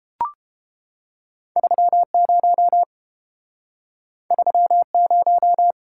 Morse code, heard as a keyed tone, about 30 wpm, 700 Hz. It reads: 30 30